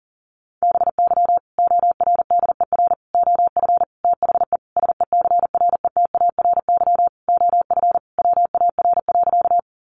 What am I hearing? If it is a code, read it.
Morse code, heard as a keyed tone, about 40 wpm, 700 Hz. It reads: BY ORDER OF THE SECRETARY OF WAR.
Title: Sentences from Top 1300 Words 40wpm